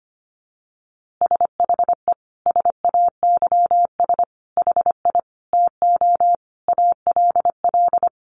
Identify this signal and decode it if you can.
Morse code, heard as a keyed tone, about 25 wpm, 700 Hz. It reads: SHE SAYS HI TO ALL
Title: Easy Phrases 25wpm